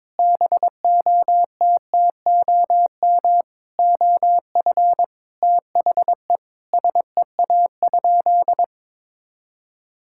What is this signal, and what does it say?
Morse code, heard as a keyed tone, about 22 wpm, 700 Hz. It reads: BOTTOM OF THE SEA?
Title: Sentences from Top 1000 Words - Review Part 6 - 22wpm